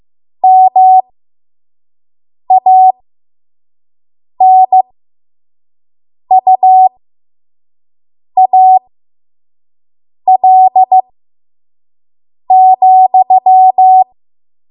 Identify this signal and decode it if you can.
Morse code, heard as a keyed tone, about 15 wpm, spaced as for 5 wpm, 750 Hz. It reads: MANUAL,
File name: Morse / 05WPM